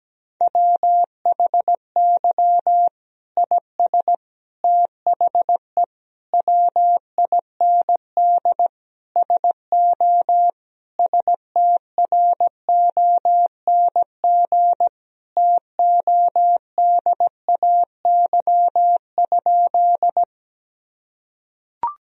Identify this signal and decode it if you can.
Morse code, heard as a keyed tone, about 17 wpm, 700 Hz. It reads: WHY IS THE WIND SO STRONG TODAY?
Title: Sentences from Top 1000 Words - Review Part 2 - 17wpm